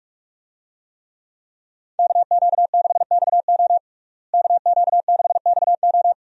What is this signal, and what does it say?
Morse code, heard as a keyed tone, about 46 wpm, 700 Hz. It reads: KQ6XO KQ6XO
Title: USA Call Signs - 5 Characters - Repeated 2x - 45wpm